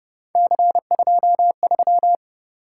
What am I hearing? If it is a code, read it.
Morse code, heard as a keyed tone, about 31 wpm, 700 Hz. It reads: C23
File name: Speed-Racing - Three Letter-Number Combinations - 25wpm